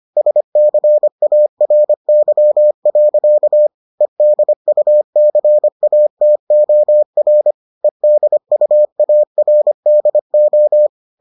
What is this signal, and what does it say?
Morse code, heard as a keyed tone, about 25 wpm, 600 Hz. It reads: SCARY. EDUCATOR EDUARDO